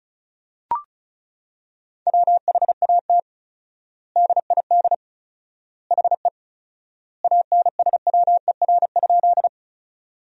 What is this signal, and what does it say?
Morse code, heard as a keyed tone, about 36 wpm, 700 Hz. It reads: WHAT DID HE ANSWER?
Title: 4x Word-Spacing - Sentences from Top 300 Words - 35wpm